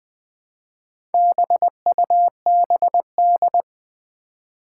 Morse code, heard as a keyed tone, about 20 wpm, 700 Hz. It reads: BUBD
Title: Four Challenging Letter-Number Combinations 20wpm